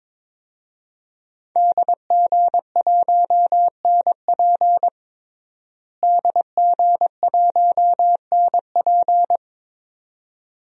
Morse code, heard as a keyed tone, about 22 wpm, 700 Hz. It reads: DG1NP DG1NP
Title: DXCC-100 Call Signs - 5 Characters - Repeated 2x - 22wpm